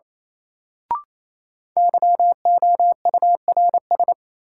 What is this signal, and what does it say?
Morse code, heard as a keyed tone, about 28 wpm, 700 Hz. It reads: YOURS
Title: Sentences from Top 1500 Words - New Word Drill - 28wpm